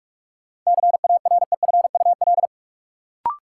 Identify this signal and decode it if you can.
Morse code, heard as a keyed tone, about 46 wpm, 700 Hz. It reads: CAREFUL
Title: Sentences from Top 1600 Words - New Word Drill Alt - 45wpm